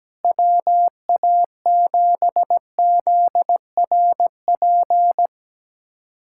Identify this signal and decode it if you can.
Morse code, heard as a keyed tone, about 17 wpm, 700 Hz. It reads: WA7ZRP
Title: USA Callsign DE Callsign - 6 Characters - 17wpm